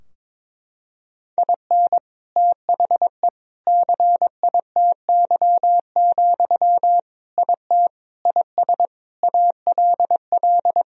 Morse code, heard as a keyed tone, about 22 wpm, 700 Hz. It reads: IN THE CITY, IT IS ALL
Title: Sentences from Top 1000 Words - Review Part 3 - 22wpm